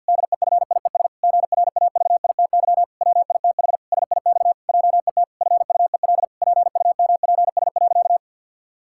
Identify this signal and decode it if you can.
Morse code, heard as a keyed tone, about 50 wpm, 700 Hz. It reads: DEFIES GRAVITY WITH SIX JET FUEL PUMPS.
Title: Pangram Challenge v2 - 50wpm